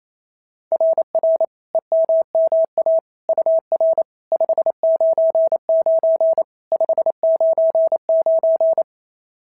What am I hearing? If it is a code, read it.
Morse code, heard as a keyed tone, about 27 wpm, 650 Hz. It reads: RR EMMA UR 599 599
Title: SOTA QSO Practice 28wpm